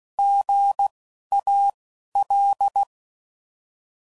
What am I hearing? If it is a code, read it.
Morse code, heard as a keyed tone, about 16 wpm, spaced as for 11 wpm, 800 Hz. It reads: GAL
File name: Prefixes / By First Letter